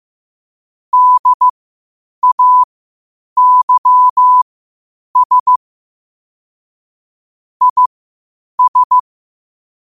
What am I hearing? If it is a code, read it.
Morse code, heard as a keyed tone, about 15 wpm, spaced as for 8 wpm, 1000 Hz. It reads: DAYS IS
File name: qotdfiles / 8wpm